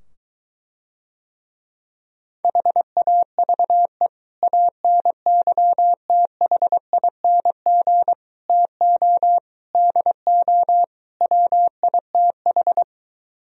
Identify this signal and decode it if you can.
Morse code, heard as a keyed tone, about 23 wpm, 700 Hz. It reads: HAVE ANYTHING TO DO WITH